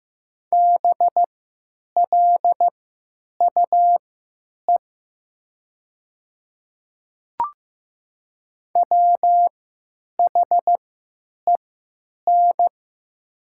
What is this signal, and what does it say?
Morse code, heard as a keyed tone, about 15 wpm, spaced as for 9 wpm, 700 Hz. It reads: BLUE WHEN
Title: 3x Character Spacing - Top 500 Words - 15wpm